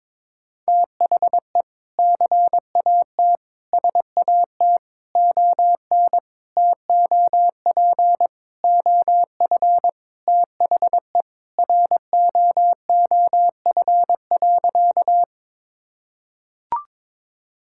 Morse code, heard as a keyed tone, about 22 wpm, 700 Hz. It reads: THE CAT SAT ON TOP OF THE ROOF.
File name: practice / 22wpm / Sentences with 3-Word Phrases - 22wpm